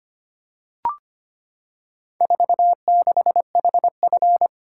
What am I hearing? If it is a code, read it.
Morse code, heard as a keyed tone, about 25 wpm, 700 Hz. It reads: 46HF
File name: practice / 25wpm / Four Letter-Number Combinations - 25wpm